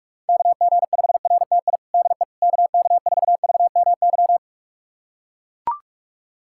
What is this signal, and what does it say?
Morse code, heard as a keyed tone, about 41 wpm, 700 Hz. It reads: KG5RTI DE KK4VMY
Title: USA Callsign DE Callsign 45wpm